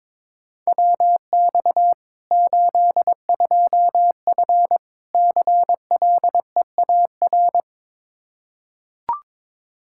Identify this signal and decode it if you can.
Morse code, heard as a keyed tone, about 22 wpm, 700 Hz. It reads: WX 82F CLEAR
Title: Practice QSO - Exchange 2 - Info - 22wpm